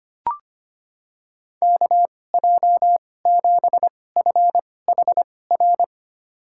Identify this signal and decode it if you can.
Morse code, heard as a keyed tone, about 25 wpm, spaced as for 18 wpm, 700 Hz. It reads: KJ7FHR
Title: USA Call Signs - 6 Characters - 2x Character Spacing - 25wpm